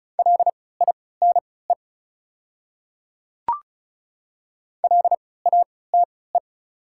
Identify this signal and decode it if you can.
Morse code, heard as a keyed tone, about 34 wpm, spaced as for 20 wpm, 690 Hz. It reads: LINE LATE
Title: 3x Character Spacing - Top 500 Words - 35wpm